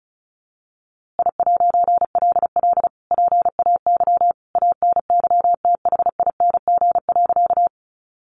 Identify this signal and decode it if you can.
Morse code, heard as a keyed tone, about 34 wpm, 700 Hz. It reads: I'LL PAY ANYTHING.
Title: A Sound Of Thunder 35wpm